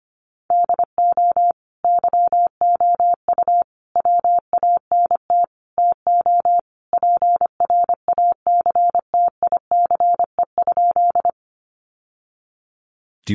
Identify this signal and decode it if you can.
Morse code, heard as a keyed tone, about 25 wpm, 700 Hz. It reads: DO YOU WANT TO PRACTICE?
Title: Sentences from Top 700 Words 25wpm